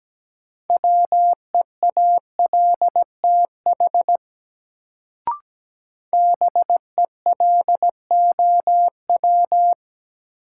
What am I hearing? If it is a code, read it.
Morse code, heard as a keyed tone, about 17 wpm, 700 Hz. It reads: WEALTH BELOW
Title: Sentences from Top 1200 Words - New Word Drill Alt - 17wpm